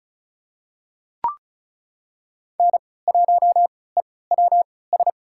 Morse code, heard as a keyed tone, about 36 wpm, spaced as for 20 wpm, 700 Hz. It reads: N1EWS
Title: USA Call Signs - 5 Characters - 3x Character Spacing - 35wpm